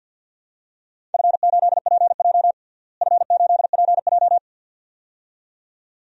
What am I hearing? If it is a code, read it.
Morse code, heard as a keyed tone, about 52 wpm, 700 Hz. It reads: F8PJ F8PJ
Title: DXCC-100 Call Signs - 4 Characters - Repeated 2x - 50wpm